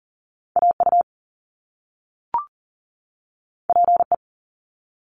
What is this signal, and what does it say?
Morse code, heard as a keyed tone, about 40 wpm, 700 Hz. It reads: AU PE
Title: Top Two Letter Combinations in Words 40wpm